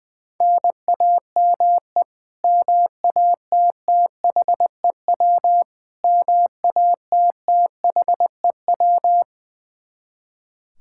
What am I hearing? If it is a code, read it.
Morse code, heard as a keyed tone, about 21 wpm, 700 Hz. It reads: NAME MATTHEW MATTHEW
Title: K1USN SST QSO Practice 20wpm